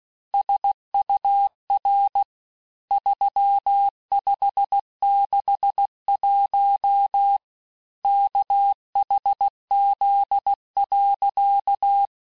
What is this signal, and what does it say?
Morse code, heard as a keyed tone, about 16 wpm, 800 Hz. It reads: SUR 3561 KHZ.
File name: cours-du-lundi files / cours-1592--teste-chiffre-vitesse-15mpm